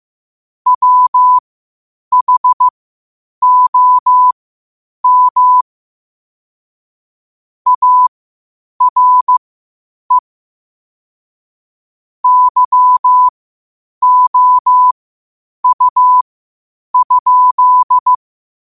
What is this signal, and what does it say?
Morse code, heard as a keyed tone, about 15 wpm, spaced as for 8 wpm, 1000 Hz. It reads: WHOM ARE YOU?
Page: Quotes for Sat, 16 Aug 2025 in Morse Code at 8 words per minute.